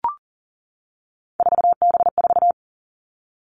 Morse code, heard as a keyed tone, about 39 wpm, 690 Hz. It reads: 4B4
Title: Three Challenging Letter-Number Combinations 40wpm